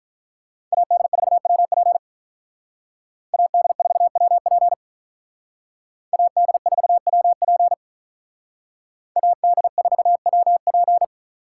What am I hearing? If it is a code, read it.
Morse code, heard as a keyed tone, about 42 wpm, 700 Hz. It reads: AD4WP AD4WP AD4WP AD4WP
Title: Speed Racing - USA 2x2 Call Signs 35wpm